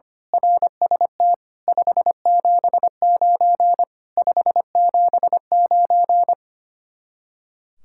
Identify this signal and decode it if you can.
Morse code, heard as a keyed tone, about 24 wpm, 700 Hz. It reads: RST 579 579
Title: Speed-Racing Common QSO Phrases Encore - 25wpm